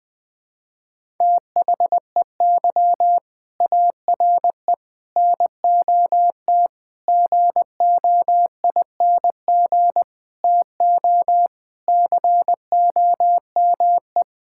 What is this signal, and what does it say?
Morse code, heard as a keyed tone, about 20 wpm, 700 Hz. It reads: THEY ARE NOT GOING TO COME
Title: Sentences with 4-Word Phrases - 20wpm